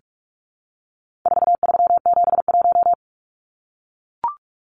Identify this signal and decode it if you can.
Morse code, heard as a keyed tone, about 44 wpm, 700 Hz. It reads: 4371
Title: Numbers - 4-Digit 45wpm